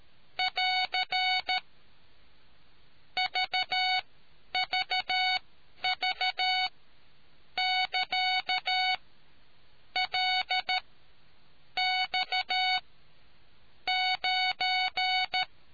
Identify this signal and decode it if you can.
Morse code, heard as a keyed tone, about 13 wpm, spaced as for 7 wpm, 740 Hz. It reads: <AR> VVV<KA>LX9